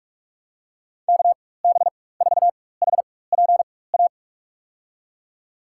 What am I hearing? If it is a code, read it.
Morse code, heard as a keyed tone, about 46 wpm, spaced as for 21 wpm, 700 Hz. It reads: KB4HPA
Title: USA Call Signs - 6 Characters - 4x Character Spacing - 45wpm